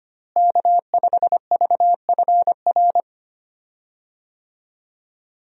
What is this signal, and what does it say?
Morse code, heard as a keyed tone, about 25 wpm, 700 Hz. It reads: K5VFR
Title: USA Call Signs - 5 Characters - Repeated 2x - 25wpm